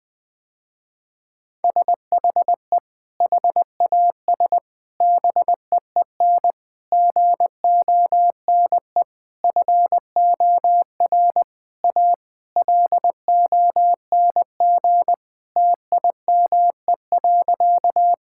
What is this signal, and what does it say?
Morse code, heard as a keyed tone, about 20 wpm, 700 Hz. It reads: SHE HAS BEEN GONE FOR A LONG TIME.
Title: Sentences with 3-Word Phrases - Spoken Beginning - 20wpm